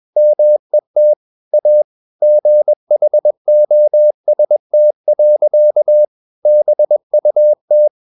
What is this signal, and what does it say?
Morse code, heard as a keyed tone, about 21 wpm, 600 Hz. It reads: MET A GHOST. BUT
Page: CW med Gnister